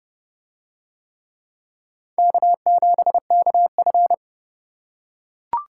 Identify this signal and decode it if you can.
Morse code, heard as a keyed tone, about 28 wpm, 690 Hz. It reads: K7KF